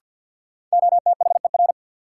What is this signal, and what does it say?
Morse code, heard as a keyed tone, about 50 wpm, 700 Hz. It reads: OTHER
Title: Sets of 2 Words - New Word Drill - 50wpm